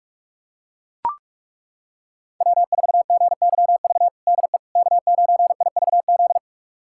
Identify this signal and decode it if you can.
Morse code, heard as a keyed tone, about 41 wpm, 700 Hz. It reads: W4GYV DE K9IVZ
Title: USA Callsign DE Callsign - 5 Characters - 45wpm